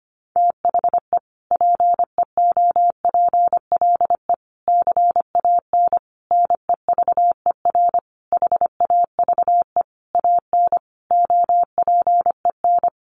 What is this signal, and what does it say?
Morse code, heard as a keyed tone, about 25 wpm, 700 Hz. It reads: THE PEOPLE CAN NEVER HAVE AN OPEN
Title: Sentences from Top 1400 Words 25wpm